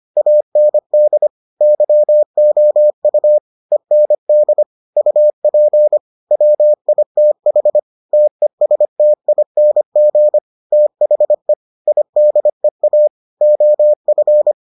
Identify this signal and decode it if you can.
Morse code, heard as a keyed tone, about 25 wpm, 600 Hz. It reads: AND YOU END UP WITH TESTING THE IDEA OF